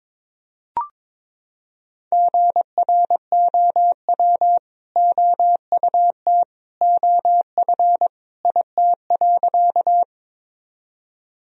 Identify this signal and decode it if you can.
Morse code, heard as a keyed tone, about 22 wpm, 700 Hz. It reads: GROW OUT OF IT.
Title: Sentences from Top 1000 Words - Review Part 1 - 22wpm